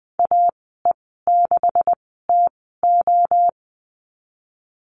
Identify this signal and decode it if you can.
Morse code, heard as a keyed tone, about 20 wpm, spaced as for 14 wpm, 700 Hz. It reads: AE6TO
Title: 128 Lesson 21g - Number 6 - Call Signs 20wpm